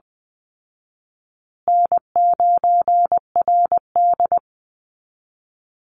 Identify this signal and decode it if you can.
Morse code, heard as a keyed tone, about 20 wpm, 700 Hz. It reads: N9RD
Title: Easy Call Signs Repeated 3x - 20wpm